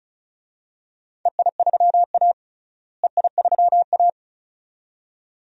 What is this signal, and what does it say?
Morse code, heard as a keyed tone, about 34 wpm, 700 Hz. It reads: EI3A EI3A
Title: DXCC-100 Call Signs - 4 Characters - Repeated 2x - 35wpm